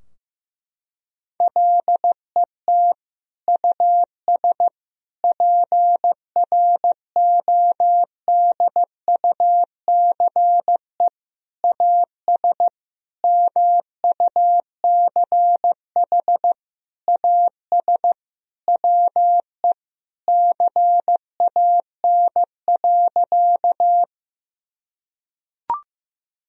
Morse code, heard as a keyed tone, about 15 wpm, 700 Hz. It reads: LET US PRODUCE AS MUCH AS WE CAN.
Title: Sentences from Top 1000 Words - Review Part 3 - 15wpm